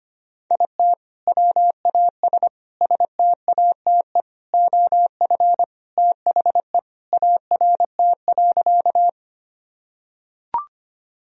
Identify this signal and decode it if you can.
Morse code, heard as a keyed tone, about 25 wpm, 700 Hz. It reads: IT WAS STATE OF THE ART.
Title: Sentences from Top 1000 Words - Review Part 3 - 25wpm